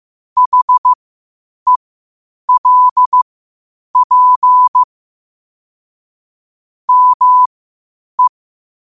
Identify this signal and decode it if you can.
Morse code, heard as a keyed tone, about 15 wpm, spaced as for 8 wpm, 1000 Hz. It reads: HELP ME